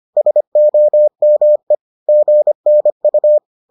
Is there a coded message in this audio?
Yes: Morse code from 0.2 to 3.4 s, heard as a keyed tone, about 25 wpm, 600 Hz. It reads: SOME GNU